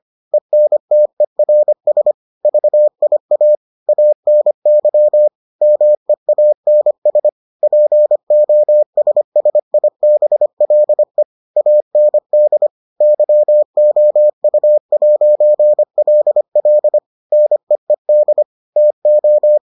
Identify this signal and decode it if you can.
Morse code, heard as a keyed tone, about 25 wpm, 600 Hz. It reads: ENTERS VIA ANY MEANS POSSIBLE AND YOU'LL NEED TO